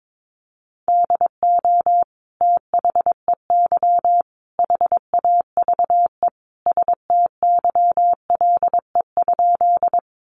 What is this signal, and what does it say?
Morse code, heard as a keyed tone, about 22 wpm, 700 Hz. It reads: DO THEY HAVE STYLE?
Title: Sentences from Top 800 Words 22wpm